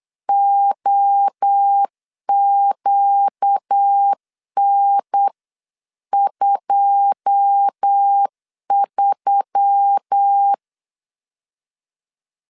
Morse code, heard as a keyed tone, about 8 wpm, 800 Hz. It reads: OQN23
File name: CW / DET / 1 9 1500C S C 8wpm 5653